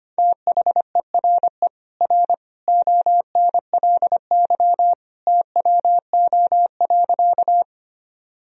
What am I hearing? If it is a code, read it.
Morse code, heard as a keyed tone, about 25 wpm, 700 Hz. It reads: THERE R ONLY TWO.
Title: Sentences from Top 200 Words and Abbreviations 25wpm